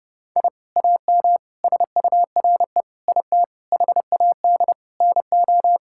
Morse code, heard as a keyed tone, about 32 wpm, 700 Hz. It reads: I AM SURE IT HAD NO
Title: Sentences from Top 1000 Words - Review Part 6 - 30wpm